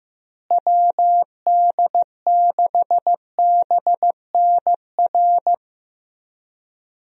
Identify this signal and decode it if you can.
Morse code, heard as a keyed tone, about 15 wpm, 700 Hz. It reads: WD6BNR
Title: USA Call Signs - 6 Characters - Repeated 3x - 15wpm